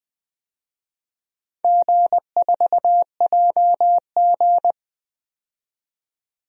Morse code, heard as a keyed tone, about 20 wpm, 700 Hz. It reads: G4JG
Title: DXCC-100 Call Signs - 4 Characters - 20wpm